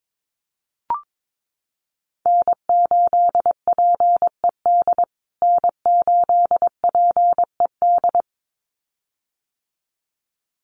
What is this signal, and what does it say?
Morse code, heard as a keyed tone, about 22 wpm, 700 Hz. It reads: N8PED N8PED
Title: USA Call Signs Repeated 2x 22wpm